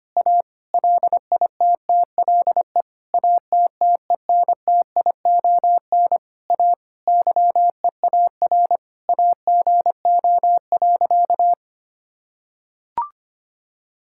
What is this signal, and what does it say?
Morse code, heard as a keyed tone, about 25 wpm, 700 Hz. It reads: A LITTLE ATTENTION A YEAR AGO.
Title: Sentences from Top 1000 Words - Review Part 6 - 25wpm